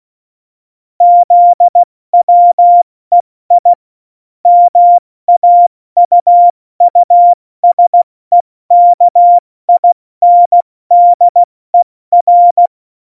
Morse code, heard as a keyed tone, about 16 wpm, spaced as for 14 wpm, 700 Hz. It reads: ZWEI MAUUSEKINDER
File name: Morsekurs / Witze / Witz 26. Juni 2018, 70 F80